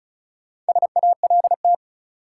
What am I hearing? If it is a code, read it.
Morse code, heard as a keyed tone, about 36 wpm, 700 Hz. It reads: SALT
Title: Sets of 5 Words - New Word Drill - 35wpm